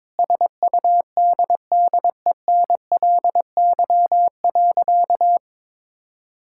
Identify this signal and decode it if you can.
Morse code, heard as a keyed tone, about 22 wpm, 700 Hz. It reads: SUDDENLY.
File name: practice / 22wpm / Sentences with 2-Word Phrases - 22wpm